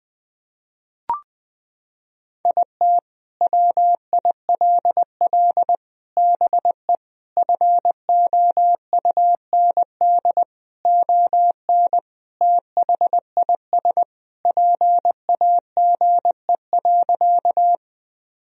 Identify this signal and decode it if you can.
Morse code, heard as a keyed tone, about 21 wpm, 700 Hz. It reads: IT WILL BE FOUND ON THIS PAGE.
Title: Sentences from Top 1000 Words - Review Part 1 - 20wpm